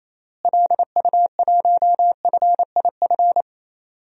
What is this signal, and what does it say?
Morse code, heard as a keyed tone, about 28 wpm, 700 Hz. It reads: LU1FIF
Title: DXCC-100 Call Signs - 6 Characters - Repeated 2x - 28wpm